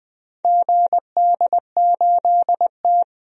Morse code, heard as a keyed tone, about 20 wpm, 700 Hz. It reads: GD8T
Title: DXCC-100 Call Signs - 4 Characters - 20wpm